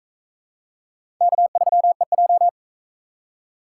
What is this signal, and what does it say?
Morse code, heard as a keyed tone, about 38 wpm, 700 Hz. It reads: K3EJ